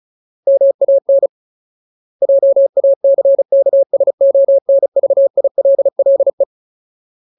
Morse code, heard as a keyed tone, about 34 wpm, 550 Hz. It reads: MAN JACKSONVILLE
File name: Ultimate Word List / 35WPM Ultimate